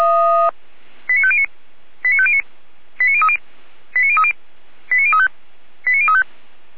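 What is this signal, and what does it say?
Five-tone selective call (ZVEI), heard as quick runs of five tones, tones of about 70 ms. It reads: 89509 89509 89030 89030 89035 89035
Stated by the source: ZVEI five-tone call received by SDR at Feuerwehr Orlamünde.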